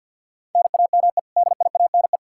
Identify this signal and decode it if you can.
Morse code, heard as a keyed tone, about 46 wpm, 700 Hz. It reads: NAME DIANE
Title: K1USN SST QSO Practice 50wpm